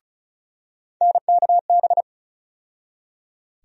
Morse code, heard as a keyed tone, about 36 wpm, 700 Hz. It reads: NKB
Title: USA Call Sign Postfixes - 3 Letters 35wpm